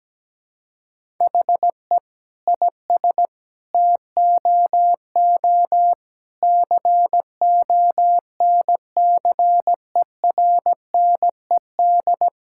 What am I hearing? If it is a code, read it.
Morse code, heard as a keyed tone, about 17 wpm, 700 Hz. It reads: HE IS TOO CONCERNED
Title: Sentences from Top 1000 Words - Review Part 6 - 17wpm